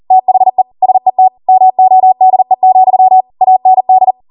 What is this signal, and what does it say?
Morse code, heard as a keyed tone, about 40 wpm, 750 Hz. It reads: THE SET MODE, AND
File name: Morse / 40WPM